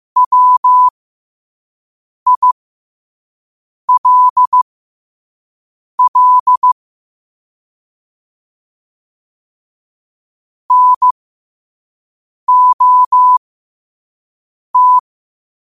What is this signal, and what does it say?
Morse code, heard as a keyed tone, about 15 wpm, spaced as for 5 wpm, 1000 Hz. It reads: WILL NOT
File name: qotdfiles / 5wpm